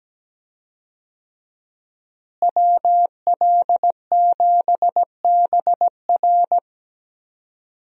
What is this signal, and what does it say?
Morse code, heard as a keyed tone, about 17 wpm, 700 Hz. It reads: WL7BR
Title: USA Call Signs - 5 Characters - Repeated 3x - 17wpm